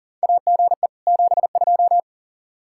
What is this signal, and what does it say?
Morse code, heard as a keyed tone, about 39 wpm, 700 Hz. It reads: AGE 72
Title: Practice QSO - Exchange 3 - More Info - 40wpm